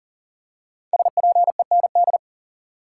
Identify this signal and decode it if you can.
Morse code, heard as a keyed tone, about 40 wpm, 700 Hz. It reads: SPEND
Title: Five Letter Words - 40wpm